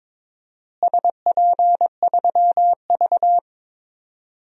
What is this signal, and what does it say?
Morse code, heard as a keyed tone, about 22 wpm, 700 Hz. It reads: SP3V
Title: DXCC-100 Call Signs - 4 Characters - Repeated 3x - 22wpm